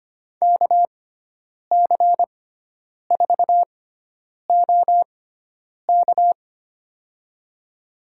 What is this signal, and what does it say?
Morse code, heard as a keyed tone, about 25 wpm, spaced as for 9 wpm, 700 Hz. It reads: KC4OK
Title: USA Call Signs - 5 Characters - 6x Character Spacing - 25wpm